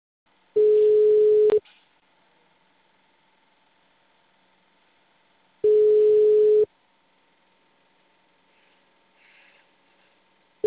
Ringback tone (Europe, 425 Hz).
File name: out- -2016- - - .88252